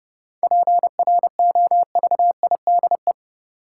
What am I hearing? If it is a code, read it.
Morse code, heard as a keyed tone, about 30 wpm, 700 Hz. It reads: PROVIDE
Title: Sentences from Top 1000 Words - Review Part 2 - New Word Drill - 30wpm